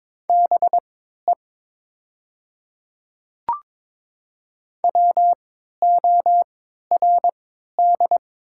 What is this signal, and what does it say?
Morse code, heard as a keyed tone, about 22 wpm, spaced as for 12 wpm, 700 Hz. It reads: BE WORD
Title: 3x Character Spacing - Top 300 Words - 22wpm